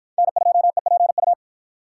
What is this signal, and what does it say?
Morse code, heard as a keyed tone, about 53 wpm, 700 Hz. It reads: N2EPU